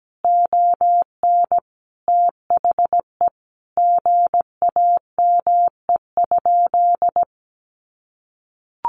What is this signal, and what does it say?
Morse code, heard as a keyed tone, about 17 wpm, 700 Hz. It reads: ON THE GAME?
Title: Sentences from Top 1500 Words 17wpm